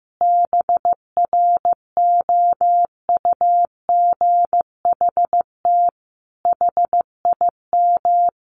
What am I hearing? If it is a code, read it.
Morse code, heard as a keyed tone, about 15 wpm, 700 Hz. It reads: BROUGHT HIM
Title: Sentences from Top 700 Words 15wpm